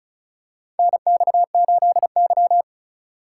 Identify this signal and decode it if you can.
Morse code, heard as a keyed tone, about 35 wpm, 690 Hz. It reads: NX8Y
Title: USA Call Signs - 4 Characters - 2x Character Spacing - 35wpm